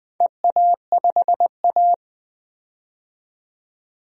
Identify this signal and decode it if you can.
Morse code, heard as a keyed tone, about 20 wpm, 700 Hz. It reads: EA5A
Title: DXCC-100 Call Signs - 4 Characters - 20wpm